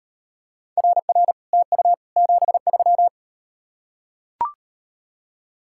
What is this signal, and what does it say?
Morse code, heard as a keyed tone, about 38 wpm, 700 Hz. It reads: RR TU 73